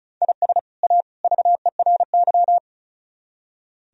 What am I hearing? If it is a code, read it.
Morse code, heard as a keyed tone, about 42 wpm, 700 Hz. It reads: IS A VERY
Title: Speed-Racing - Sets of 3 Words 35wpm